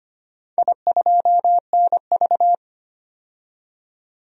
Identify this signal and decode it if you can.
Morse code, heard as a keyed tone, about 25 wpm, 700 Hz. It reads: I2NV
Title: DXCC-100 Call Signs - 4 Characters - 25wpm